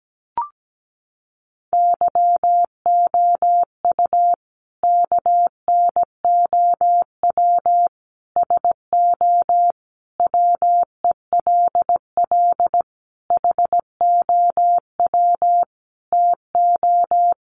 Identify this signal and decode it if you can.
Morse code, heard as a keyed tone, about 17 wpm, 700 Hz. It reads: YOU KNOW SO WELL HOW TO
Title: Sentences from Top 800 Words 17wpm